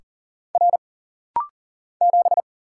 Morse code, heard as a keyed tone, about 39 wpm, 700 Hz. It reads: R 7
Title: Single Letter-Number Alt - Rapid-Fire 40wpm